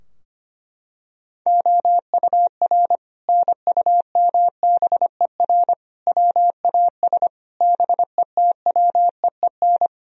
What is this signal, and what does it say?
Morse code, heard as a keyed tone, about 25 wpm, 700 Hz. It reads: OUR NUMBER WAS BETWEEN
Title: Sentences from Top 1000 Words - Review Part 2 - 25wpm